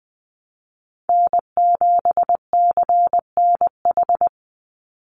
Morse code, heard as a keyed tone, about 20 wpm, 700 Hz. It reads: N7CNH
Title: NA SOTA Chaser Callsigns 20wpm